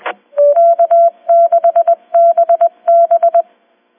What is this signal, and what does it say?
Morse code, heard as a keyed tone, about 20 wpm, 690 Hz. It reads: K6BB